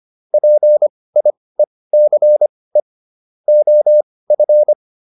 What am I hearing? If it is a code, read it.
Morse code, heard as a keyed tone, about 25 wpm, spaced as for 18 wpm, 600 Hz. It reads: PIECE OF